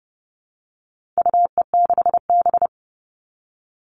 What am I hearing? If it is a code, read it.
Morse code, heard as a keyed tone, about 30 wpm, 700 Hz. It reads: UE6B
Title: All Cut Numbers - 4-Digit 30wpm